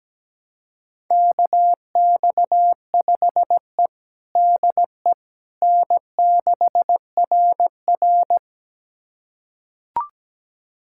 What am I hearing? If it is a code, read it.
Morse code, heard as a keyed tone, about 17 wpm, 700 Hz. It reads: KX5E DE N6RR
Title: USA Callsign DE Callsign - 4 Characters - 17wpm